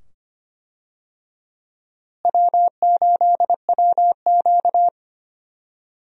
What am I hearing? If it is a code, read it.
Morse code, heard as a keyed tone, about 25 wpm, 700 Hz. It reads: W8WQ
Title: USA Call Signs - 4 Characters - Repeated 2x - 25wpm